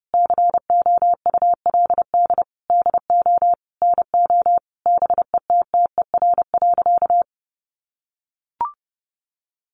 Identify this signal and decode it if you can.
Morse code, heard as a keyed tone, about 30 wpm, 700 Hz. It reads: COULD DO NO BETTER.
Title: Sentences from Top 1300 Words 30wpm